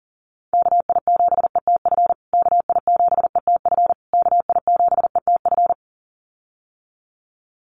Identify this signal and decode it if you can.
Morse code, heard as a keyed tone, about 38 wpm, 700 Hz. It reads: KI7ETF KI7ETF KI7ETF
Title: USA Call Signs Repeated 3x 40wpm